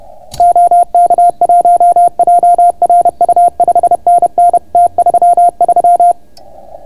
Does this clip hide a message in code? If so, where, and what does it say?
Yes: Morse code from 0.4 to 6.1 s, heard as a keyed tone, about 30 wpm, 690 Hz. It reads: OK1JRU 5NN T33